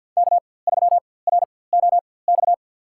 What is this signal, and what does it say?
Morse code, heard as a keyed tone, about 49 wpm, spaced as for 23 wpm, 700 Hz. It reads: K3ROX
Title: USA Call Signs - 5 Characters - 4x Character Spacing - 50wpm